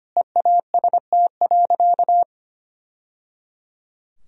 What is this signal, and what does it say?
Morse code, heard as a keyed tone, about 25 wpm, 700 Hz. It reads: EAST.
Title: Sentences from Top 1000 Words - Review Part 3 - 25wpm